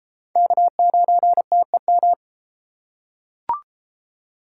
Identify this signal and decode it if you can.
Morse code, heard as a keyed tone, about 33 wpm, 700 Hz. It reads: K9TEM